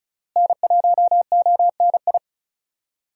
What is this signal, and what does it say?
Morse code, heard as a keyed tone, about 36 wpm, 700 Hz. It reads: N1ONI
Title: USA Call Signs - 5 Characters - Repeated 3x - 35wpm